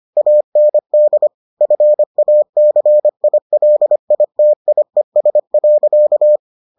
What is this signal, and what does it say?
Morse code, heard as a keyed tone, about 25 wpm, 600 Hz. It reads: AND FACILITIES.